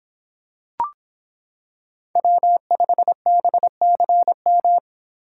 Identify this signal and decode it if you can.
Morse code, heard as a keyed tone, about 26 wpm, 700 Hz. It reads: W5BCM